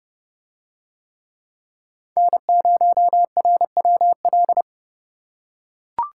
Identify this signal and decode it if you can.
Morse code, heard as a keyed tone, about 29 wpm, 700 Hz. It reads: N0RWL